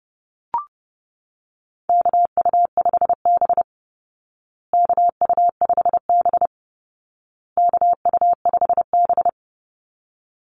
Morse code, heard as a keyed tone, about 30 wpm, 700 Hz. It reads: KU5B KU5B KU5B
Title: Call Signs Repeated 2x - 30wpm